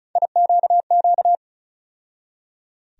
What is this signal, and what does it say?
Morse code, heard as a keyed tone, about 36 wpm, 700 Hz. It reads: IQQ
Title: USA Call Sign Postfixes - 3 Letters 35wpm